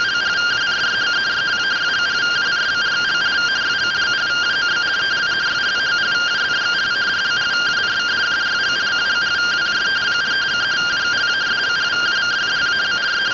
Two-tone teleprinter signal, about 170 Hz shift.